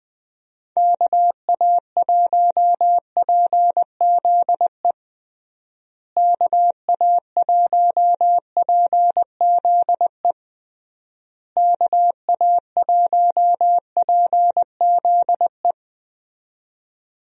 Morse code, heard as a keyed tone, about 20 wpm, 700 Hz. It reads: KA1PZE KA1PZE KA1PZE
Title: USA Call Signs - 6 Characters - Repeated 3x - 20wpm